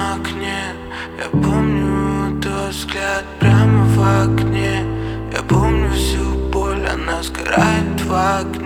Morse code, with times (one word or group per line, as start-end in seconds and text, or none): none